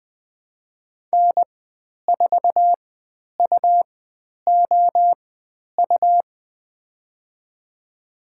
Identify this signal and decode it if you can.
Morse code, heard as a keyed tone, about 20 wpm, spaced as for 10 wpm, 700 Hz. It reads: N4UOU
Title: 109 Lesson 18g - Letter m - Call Signs 20wpm